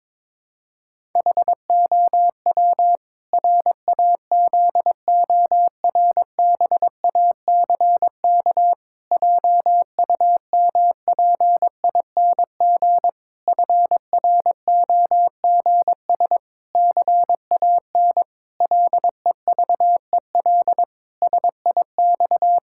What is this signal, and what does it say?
Morse code, heard as a keyed tone, about 22 wpm, 700 Hz. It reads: HOW RAZORBACK JUMPING FROGS CAN LEVEL SIX
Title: Pangram Challenge v2 - 22wpm